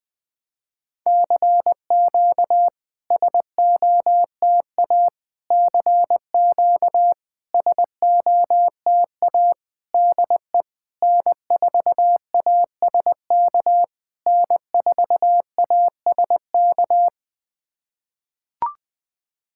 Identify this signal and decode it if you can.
Morse code, heard as a keyed tone, about 20 wpm, 690 Hz. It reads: CQ SOTA CQ SOTA DE N4ASK N4ASK
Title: SOTA QSO Practice 20wpm